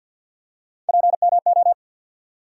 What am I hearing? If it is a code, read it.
Morse code, heard as a keyed tone, about 48 wpm, 700 Hz. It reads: PMO
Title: USA Call Sign Postfixes - 3 Letters 50wpm